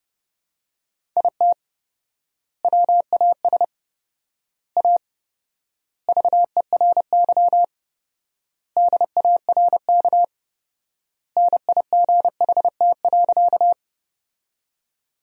Morse code, heard as a keyed tone, about 30 wpm, 700 Hz. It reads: IT WAS A VERY DARK NIGHT.
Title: 4x Word-Spacing - Sentences from Top 500 Words - 30wpm